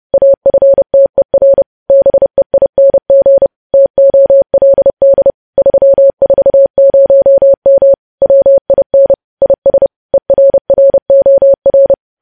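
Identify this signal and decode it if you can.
Morse code, heard as a keyed tone, about 30 wpm, 550 Hz. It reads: AFTER BEING TOLD 340M WIN IS ERROR
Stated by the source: Morse code news headlines sent at 30 wpm.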